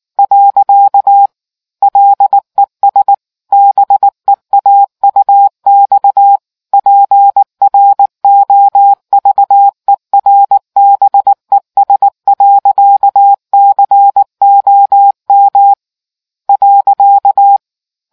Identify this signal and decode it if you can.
Morse code, heard as a keyed tone, about 19 wpm, 800 Hz. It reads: . LESBEAUXPROVERBES.COM .